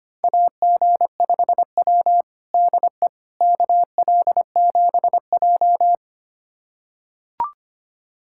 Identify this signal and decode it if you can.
Morse code, heard as a keyed tone, about 25 wpm, 700 Hz. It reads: AG5W DE KL7J
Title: USA Callsign DE Callsign - 4 Characters - 25wpm